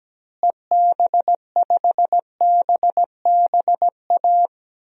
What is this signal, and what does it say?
Morse code, heard as a keyed tone, about 17 wpm, 700 Hz. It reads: EB5BBA
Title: DXCC-100 Call Signs - 6 Characters - Repeated 3x - 17wpm